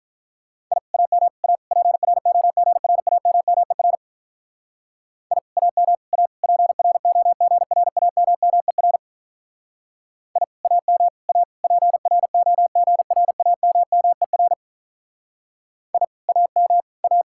Morse code, heard as a keyed tone, about 43 wpm, 700 Hz. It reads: I AM A PROGRAMMER I AM A PROGRAMMER I AM A PROGRAMMER I AM A
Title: Speed-Racing Common QSO Phrases - 35wpm